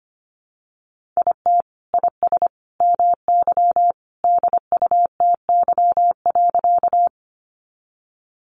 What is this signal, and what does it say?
Morse code, heard as a keyed tone, about 25 wpm, 700 Hz. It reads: IT IS MY DUTY.
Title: Sentences from Top 1500 Words 25wpm